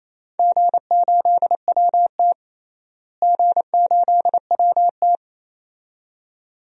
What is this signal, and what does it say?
Morse code, heard as a keyed tone, about 28 wpm, 700 Hz. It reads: G8WT G8WT
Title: DXCC-100 Call Signs - 4 Characters - Repeated 2x - 28wpm